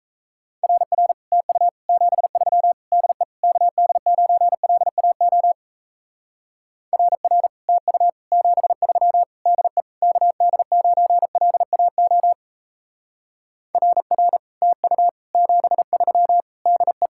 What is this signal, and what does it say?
Morse code, heard as a keyed tone, about 35 wpm, 700 Hz. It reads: RR TU 73 DE KD9LAO RR TU 73 DE KD9LAO RR TU 73 DE
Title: Speed-Racing Common QSO Phrases - 28wpm